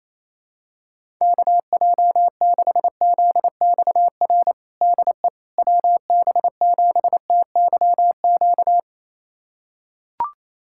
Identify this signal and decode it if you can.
Morse code, heard as a keyed tone, about 28 wpm, 700 Hz. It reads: KJ6ZXR DE WB7TYQ
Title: USA Callsign DE Callsign - 6 Characters - 28wpm